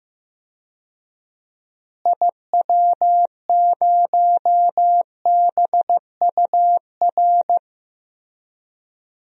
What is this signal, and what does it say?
Morse code, heard as a keyed tone, about 15 wpm, 700 Hz. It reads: IW0BUR
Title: DXCC-100 Call Signs - 6 Characters - Repeated 3x - 15wpm